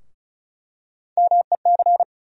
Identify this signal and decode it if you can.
Morse code, heard as a keyed tone, about 34 wpm, 700 Hz. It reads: MEC
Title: USA Call Sign Postfixes - 3 Letters 35wpm